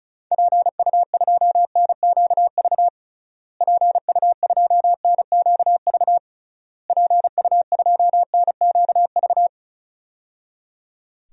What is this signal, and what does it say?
Morse code, heard as a keyed tone, about 34 wpm, 700 Hz. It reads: PU2NQV PU2NQV PU2NQV
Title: DXCC-100 Call Signs - 6 Characters - Repeated 3x - 35wpm